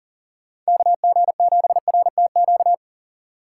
Morse code, heard as a keyed tone, about 40 wpm, 700 Hz. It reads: KG7RTQ
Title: USA Call Signs - 6 Characters - 4x Character Spacing - 40wpm